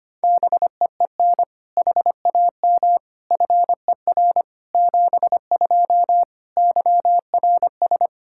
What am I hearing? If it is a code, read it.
Morse code, heard as a keyed tone, about 25 wpm, 700 Hz. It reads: BEEN HAM FER 72 YRS
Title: Practice QSO - Exchange 3 - More Info - 25wpm